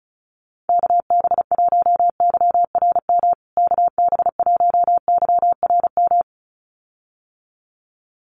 Morse code, heard as a keyed tone, about 34 wpm, 700 Hz. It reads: KB1YRM KB1YRM
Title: USA Call Signs Repeated 2x 35wpm